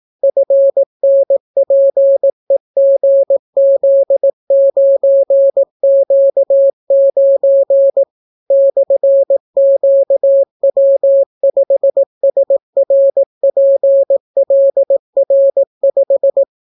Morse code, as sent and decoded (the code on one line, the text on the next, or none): ..-. -. .--. --. --.. ----. --.- ----. / -..-. --.- .-- ..... ... .-. .--. .-.. .-. .....
FNPEGZ9Q9 /QW5SRPLR5